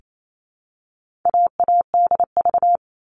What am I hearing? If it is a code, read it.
Morse code, heard as a keyed tone, about 28 wpm, 700 Hz. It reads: AADV
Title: All Cut Numbers - 4-Digit 28wpm